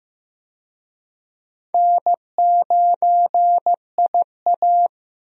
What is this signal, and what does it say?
Morse code, heard as a keyed tone, about 15 wpm, 700 Hz. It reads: N9IA
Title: USA Call Signs - 4 Characters - Repeated 2x - 15wpm